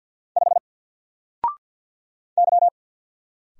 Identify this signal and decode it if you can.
Morse code, heard as a keyed tone, about 49 wpm, 700 Hz. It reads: 5 Y
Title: Single Character (Letter:Number:Punctuation:Prosign) 50wpm